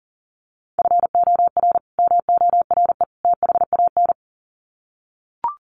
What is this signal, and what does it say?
Morse code, heard as a keyed tone, about 40 wpm, 700 Hz. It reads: FOR MORE THAN
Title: Sets of 3 Words 40wpm